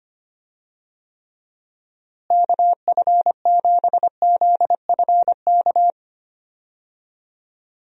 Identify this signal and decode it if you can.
Morse code, heard as a keyed tone, about 25 wpm, 700 Hz. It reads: KF7ZFK
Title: USA Call Signs - 6 Characters - Repeated 3x - 25wpm